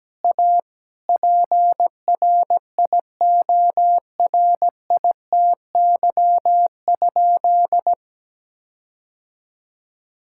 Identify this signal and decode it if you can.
Morse code, heard as a keyed tone, about 17 wpm, 700 Hz. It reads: A PRIORITY?
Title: Sentences from Top 1600 Words 17wpm